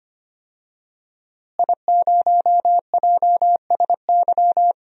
Morse code, heard as a keyed tone, about 25 wpm, 700 Hz. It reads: I0JSY
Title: DXCC-100 Call Signs - 5 Characters - Repeated 3x - 25wpm